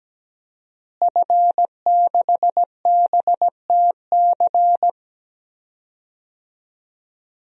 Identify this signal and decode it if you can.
Morse code, heard as a keyed tone, about 17 wpm, 700 Hz. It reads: F6BTC
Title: DXCC-100 Call Signs - 5 Characters - Repeated 2x - 17wpm